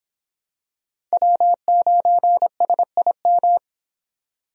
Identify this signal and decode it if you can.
Morse code, heard as a keyed tone, about 24 wpm, 700 Hz. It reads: W9SIM